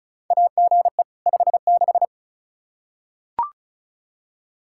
Morse code, heard as a keyed tone, about 36 wpm, 700 Hz. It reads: AGE 56
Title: K1USN SST QSO Practice 35wpm